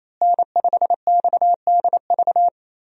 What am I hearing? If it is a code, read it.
Morse code, heard as a keyed tone, about 27 wpm, 700 Hz. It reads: N5XDV
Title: USA Call Signs - 5 Characters - Repeated 2x - 28wpm